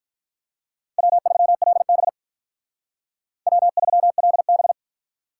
Morse code, heard as a keyed tone, about 46 wpm, 700 Hz. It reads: W3LB W3LB